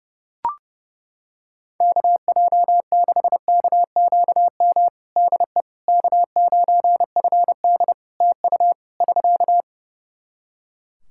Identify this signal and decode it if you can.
Morse code, heard as a keyed tone, about 29 wpm, 700 Hz. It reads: KJ6KQM DE K9FD TU <SK>
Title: K1USN SST QSO Practice 30wpm